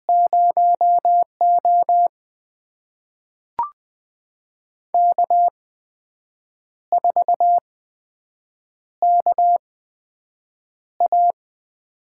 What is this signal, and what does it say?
Morse code, heard as a keyed tone, about 20 wpm, spaced as for 7 wpm, 700 Hz. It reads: 0O K4KA
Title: USA Call Signs - 4 Characters - 8x Character Spacing - 20wpm